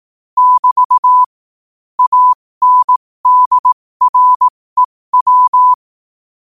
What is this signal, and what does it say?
Morse code, heard as a keyed tone, about 18 wpm, spaced as for 15 wpm, 1000 Hz. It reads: <BT> ANDREW
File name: qotdfiles / 15wpm